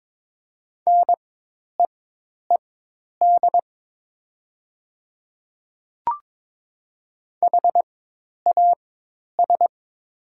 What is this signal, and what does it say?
Morse code, heard as a keyed tone, about 22 wpm, spaced as for 10 wpm, 700 Hz. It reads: NEED HAS
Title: 4x Character Spacing - Top 200 Words - 22wpm